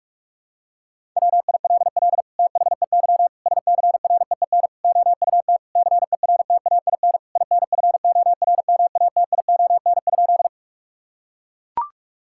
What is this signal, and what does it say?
Morse code, heard as a keyed tone, about 41 wpm, 700 Hz. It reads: WILL THEY SCREEN OUT CERTAIN INFORMATION?
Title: Sentences from Top 1000 Words - Review Part 6 - 45wpm